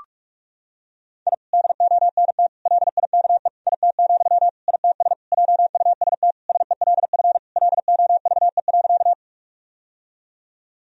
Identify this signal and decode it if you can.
Morse code, heard as a keyed tone, about 42 wpm, 700 Hz. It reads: I DONT LIKE IT, ITS JUST SELF LOVE.
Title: Sentences from Top 1000 Words - Review Part 1 - 45wpm